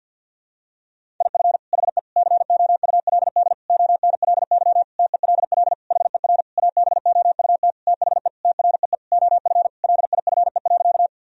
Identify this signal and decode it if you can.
Morse code, heard as a keyed tone, about 52 wpm, 700 Hz. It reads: IF HE COULD ONLY TELL HER ABOUT THE TREE OF LIFE.
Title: Sentences from Top 1000 Words - Review Part 1 - 50wpm